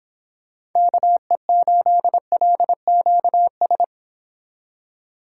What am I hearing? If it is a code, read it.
Morse code, heard as a keyed tone, about 26 wpm, 700 Hz. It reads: KE8LQS